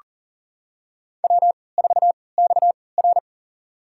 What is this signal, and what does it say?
Morse code, heard as a keyed tone, about 39 wpm, spaced as for 22 wpm, 700 Hz. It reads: W4XR
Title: USA Call Signs - 4 Characters - 3x Character Spacing - 40wpm